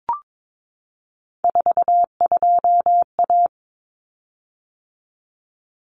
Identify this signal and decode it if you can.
Morse code, heard as a keyed tone, about 22 wpm, 700 Hz. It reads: 42A
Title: Common Cut Numbers - 3-Digit 22wpm